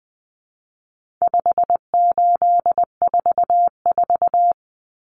Morse code, heard as a keyed tone, about 19 wpm, 700 Hz. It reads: 5844
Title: Common Cut Numbers - 4-Digit 20wpm